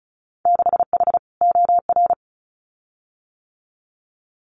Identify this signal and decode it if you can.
Morse code, heard as a keyed tone, about 36 wpm, 700 Hz. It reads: 6H OR
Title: Winter Field Day Exchanges Encore 35wpm